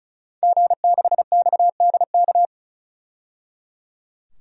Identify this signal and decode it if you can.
Morse code, heard as a keyed tone, about 36 wpm, 700 Hz. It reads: G6XDK
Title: DXCC-100 Call Signs - 5 Characters - 35wpm